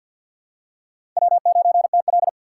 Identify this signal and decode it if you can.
Morse code, heard as a keyed tone, about 50 wpm, 700 Hz. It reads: W9TL
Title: USA Call Signs - 3x Character Spacing - 50wpm